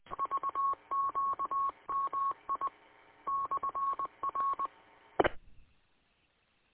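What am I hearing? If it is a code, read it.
Morse code, heard as a keyed tone, about 20 wpm, 1100 Hz. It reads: VQMI /R
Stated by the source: Morse code received from the W8UM audio archive on 145.230 MHz.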